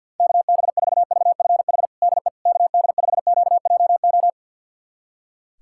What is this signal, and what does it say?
Morse code, heard as a keyed tone, about 52 wpm, 700 Hz. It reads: KB4VFH DE KD5QJO
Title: USA Callsign DE Callsign 50wpm